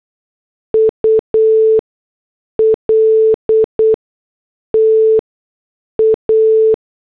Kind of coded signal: Morse code